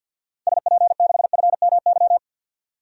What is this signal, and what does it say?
Morse code, heard as a keyed tone, about 50 wpm, 700 Hz. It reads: SP6FMY
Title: DXCC-100 Call Signs - 6 Characters - 50wpm